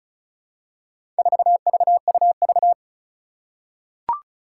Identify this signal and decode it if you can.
Morse code, heard as a keyed tone, about 34 wpm, 700 Hz. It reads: 4VUV
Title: Four Challenging Letter-Number Combinations 35wpm